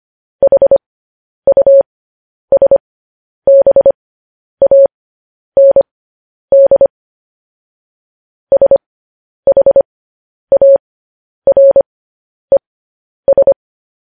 Morse code, heard as a keyed tone, about 25 wpm, spaced as for 10 wpm, 550 Hz. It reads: HUSBAND SHARES
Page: news-headlines-in-morse-code-at-10-wpm